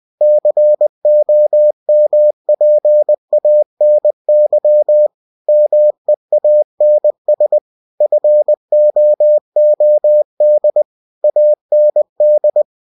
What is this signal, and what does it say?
Morse code, heard as a keyed tone, about 20 wpm, 600 Hz. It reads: COMPANY MEANS FOOD AND
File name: cw-books / winnie-the-pooh / winnie-20wpm-page-0013